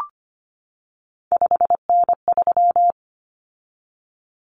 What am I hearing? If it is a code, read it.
Morse code, heard as a keyed tone, about 25 wpm, 700 Hz. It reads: 5N3
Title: Common Cut Numbers - 3-Digit 25wpm